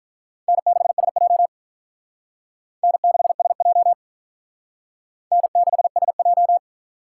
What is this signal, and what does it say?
Morse code, heard as a keyed tone, about 48 wpm, 700 Hz. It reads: N6SJ N6SJ N6SJ